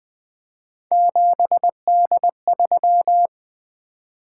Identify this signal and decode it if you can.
Morse code, heard as a keyed tone, about 20 wpm, 700 Hz. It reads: 7D3
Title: Three Letter-Number Combinations - 20wpm